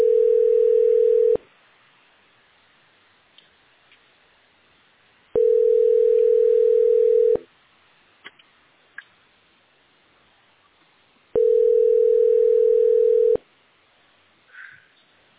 Ringback tone (North America), 440 + 480 Hz.